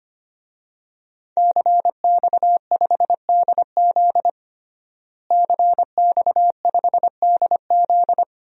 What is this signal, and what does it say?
Morse code, heard as a keyed tone, about 25 wpm, 700 Hz. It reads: CX5DZ CX5DZ
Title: DXCC-100 Call Signs - 5 Characters - Repeated 3x - 25wpm